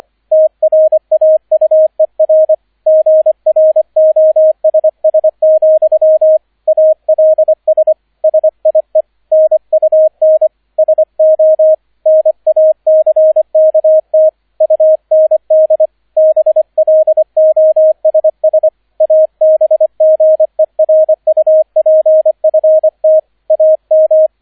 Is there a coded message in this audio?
Yes: Morse code from 0.3 to 24.3 s, heard as a keyed tone, about 25 wpm, 630 Hz, with radio noise. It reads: TRAUER GROSS, ALS SIE NUN SO NACKT UND BLOSS ABGERUPFT AM